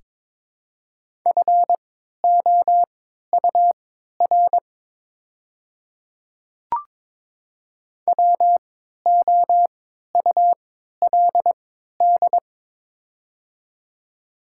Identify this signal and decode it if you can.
Morse code, heard as a keyed tone, about 22 wpm, spaced as for 13 wpm, 700 Hz. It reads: FOUR WOULD
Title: 3x Character Spacing - Top 300 Words - 22wpm